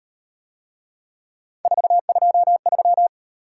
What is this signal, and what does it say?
Morse code, heard as a keyed tone, about 38 wpm, 700 Hz. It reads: 423